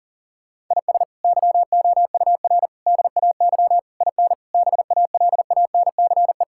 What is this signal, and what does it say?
Morse code, heard as a keyed tone, about 40 wpm, 690 Hz. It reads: IS YOUR DAY IN BALANCE
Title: Sentences from Top 1000 Words - Review Part 6 - 40wpm